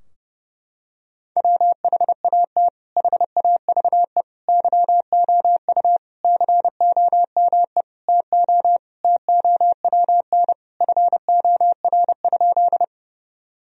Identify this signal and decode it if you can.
Morse code, heard as a keyed tone, about 32 wpm, 700 Hz. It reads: WHAT HAVE YOU COME TO TOWN FOR?
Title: Sentences from Top 1000 Words - Review Part 2 - 30wpm